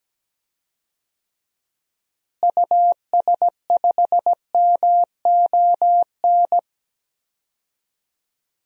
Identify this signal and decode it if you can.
Morse code, heard as a keyed tone, about 17 wpm, 700 Hz. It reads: US5MON
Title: DXCC-100 Call Signs - 6 Characters - Repeated 3x - 17wpm